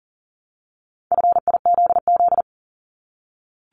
Morse code, heard as a keyed tone, about 41 wpm, 700 Hz. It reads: FIZZ
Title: Multi-letter Phonemes Repeated 3x 40wpm